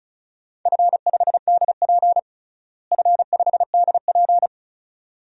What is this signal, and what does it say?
Morse code, heard as a keyed tone, about 36 wpm, 700 Hz. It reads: F5DP F5DP
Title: DXCC-100 Call Signs - 4 Characters - Repeated 2x - 35wpm